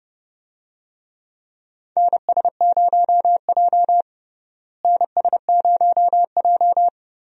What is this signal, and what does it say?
Morse code, heard as a keyed tone, about 30 wpm, 700 Hz. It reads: NS0J NS0J
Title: USA Call Signs - 4 Characters - Repeated 2x - 30wpm